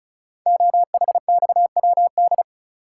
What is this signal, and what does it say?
Morse code, heard as a keyed tone, about 36 wpm, 700 Hz. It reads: OHXWD
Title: Code Groups - 5 Letters - 35wpm